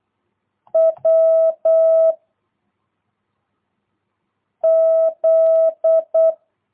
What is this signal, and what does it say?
Morse code, heard as a keyed tone, about 8 wpm, 660 Hz. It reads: W Z
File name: jh8cbh / 2022cw